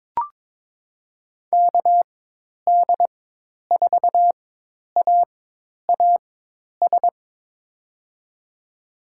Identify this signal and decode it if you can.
Morse code, heard as a keyed tone, about 21 wpm, spaced as for 10 wpm, 700 Hz. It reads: KD4AAS
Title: USA Call Signs - 6 Characters - 4x Character Spacing - 22wpm